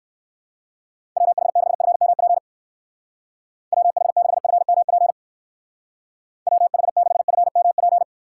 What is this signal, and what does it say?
Morse code, heard as a keyed tone, about 55 wpm, 700 Hz. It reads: WH6FMP WH6FMP WH6FMP
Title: Speed Racing - USA 2x3 Call Signs 45wpm